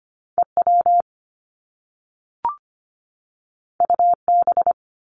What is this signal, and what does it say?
Morse code, heard as a keyed tone, about 25 wpm, 700 Hz. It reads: EW UB
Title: Top Two Letter Combinations in Words 25wpm